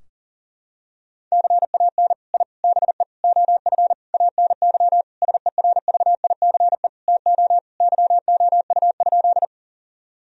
Morse code, heard as a keyed tone, about 40 wpm, 700 Hz. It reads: CAN I BE OF ANY SERVICE TO YOU?
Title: Sentences from Top 1000 Words - Review Part 4 - 40wpm